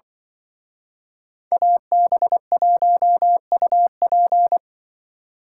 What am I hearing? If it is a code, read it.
Morse code, heard as a keyed tone, about 24 wpm, 700 Hz. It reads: AB1UP
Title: Speed Racing - USA 2x2 Call Signs 20wpm